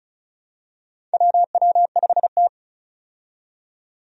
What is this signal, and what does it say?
Morse code, heard as a keyed tone, about 34 wpm, 700 Hz. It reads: WW5T
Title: USA Call Signs - 4 Characters - 35wpm